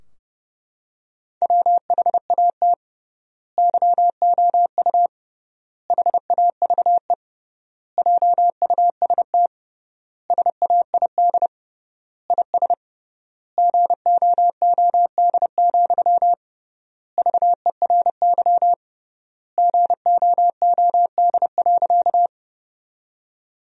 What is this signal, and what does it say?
Morse code, heard as a keyed tone, about 30 wpm, 700 Hz. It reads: WHAT YOU HAVE JUST SAID IS GOOD, VERY GOOD.
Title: 3x Word-Spacing - Sentences from Top 200 Words - 30wpm